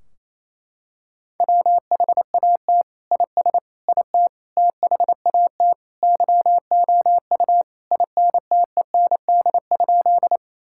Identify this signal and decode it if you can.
Morse code, heard as a keyed tone, about 28 wpm, 700 Hz. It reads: WHAT IS IT THAT YOU INTEND?
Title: Sentences from Top 1600 Words 28wpm